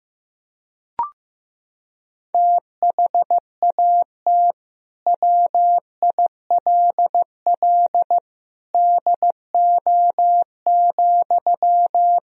Morse code, heard as a keyed tone, about 15 wpm, 700 Hz. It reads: THAT WILL DO,
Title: Sentences from Top 100 Words and Abbreviations 15wpm